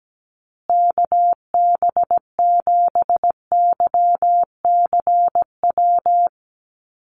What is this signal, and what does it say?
Morse code, heard as a keyed tone, about 17 wpm, 700 Hz. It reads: KB7YCW
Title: USA Call Signs Repeated 2x 17wpm